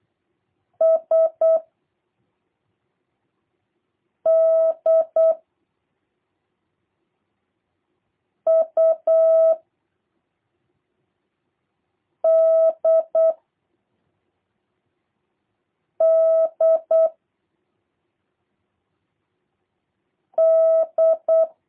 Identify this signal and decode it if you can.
Morse code, heard as a keyed tone, about 8 wpm, 660 Hz. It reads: S D U D D D
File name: jh8cbh / 2023cw / review1-3